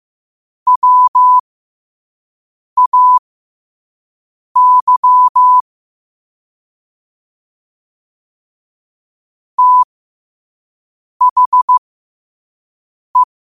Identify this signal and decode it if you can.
Morse code, heard as a keyed tone, about 15 wpm, spaced as for 5 wpm, 1000 Hz. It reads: WAY THE